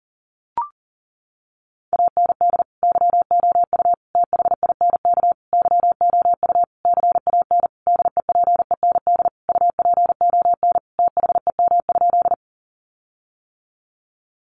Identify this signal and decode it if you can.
Morse code, heard as a keyed tone, about 40 wpm, 700 Hz. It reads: AND YOU THINK YOU CAN DEPEND UPON THEM?
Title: Sentences from Top 1300 Words 40wpm